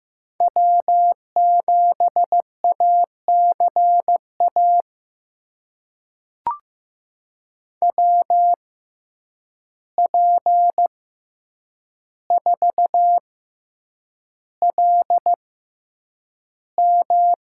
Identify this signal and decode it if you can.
Morse code, heard as a keyed tone, about 15 wpm, spaced as for 8 wpm, 700 Hz. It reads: W7ACA WP4LM
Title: USA Call Signs - 5 Characters - 6x Character Spacing - 15wpm